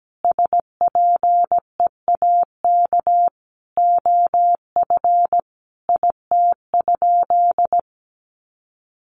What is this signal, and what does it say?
Morse code, heard as a keyed tone, about 17 wpm, 700 Hz. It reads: SPEAK OF IT?
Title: Sentences from Top 1100 Words 17wpm